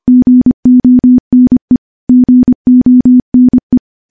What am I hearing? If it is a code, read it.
Morse code, heard as a keyed tone, about 25 wpm, 250 Hz. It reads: GONE GONE
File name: morse / morse-training-25WPM